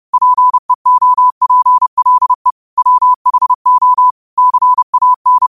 Morse code, heard as a keyed tone, about 30 wpm, 1000 Hz. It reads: PEOPLE WHO CAN
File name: qotdfiles / 30wpm